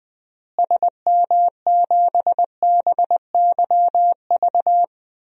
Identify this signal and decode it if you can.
Morse code, heard as a keyed tone, about 20 wpm, 700 Hz. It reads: SM7BYV
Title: DXCC-100 Call Signs - 6 Characters - Repeated 3x - 20wpm